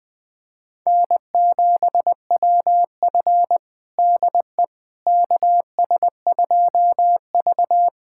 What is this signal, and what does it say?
Morse code, heard as a keyed tone, about 20 wpm, 700 Hz. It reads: N7WF DE KS2V
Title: USA Callsign DE Callsign - 4 Characters - 20wpm